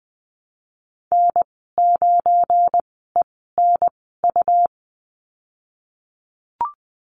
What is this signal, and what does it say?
Morse code, heard as a keyed tone, about 20 wpm, spaced as for 14 wpm, 700 Hz. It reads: N9ENU
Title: 186 Lesson 30g - Letter b - Call Signs 20wpm